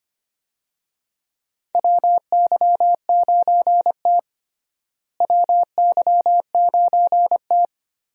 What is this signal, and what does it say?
Morse code, heard as a keyed tone, about 25 wpm, 700 Hz. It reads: WY9T WY9T
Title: USA Call Signs - 4 Characters - Repeated 3x - 25wpm